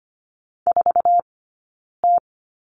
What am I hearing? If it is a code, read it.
Morse code, heard as a keyed tone, about 25 wpm, 700 Hz. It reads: 4 T
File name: course / 25:10wpm / 092 Lesson 16d - Letter c - Two Characters 25wpm